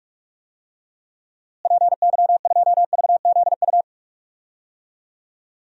Morse code, heard as a keyed tone, about 46 wpm, 700 Hz. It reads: PY2VZU
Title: DXCC-100 Call Signs - 6 Characters - Repeated 3x - 45wpm